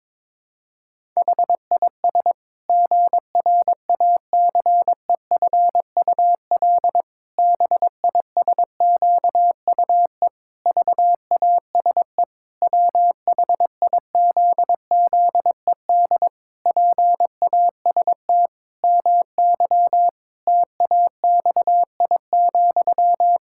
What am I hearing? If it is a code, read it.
Morse code, heard as a keyed tone, about 22 wpm, 700 Hz. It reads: HIS GRACEFUL BISQUE VASE WHIZZED PAST MY TAXI,
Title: Pangram Challenge v2 - 22wpm